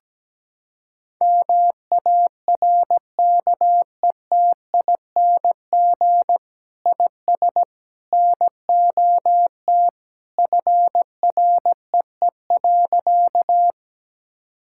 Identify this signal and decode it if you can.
Morse code, heard as a keyed tone, about 17 wpm, 700 Hz. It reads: MARKETING IS NOT FREE.
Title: Sentences from Top 1000 Words - Review Part 5 - 17wpm